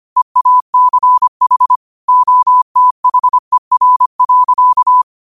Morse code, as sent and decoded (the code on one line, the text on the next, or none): . .- -.-. .... / --- - .... .-. .-.-.-
EACH OTHER.